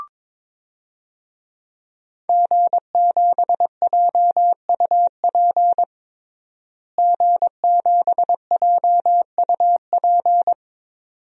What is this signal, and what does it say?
Morse code, heard as a keyed tone, about 22 wpm, 700 Hz. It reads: G7JUP G7JUP
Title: DXCC-100 Call Signs - 5 Characters - Repeated 2x - 22wpm